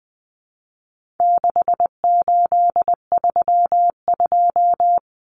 Morse code, heard as a keyed tone, about 20 wpm, 700 Hz. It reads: 6832
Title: Numbers - 4-Digit 20wpm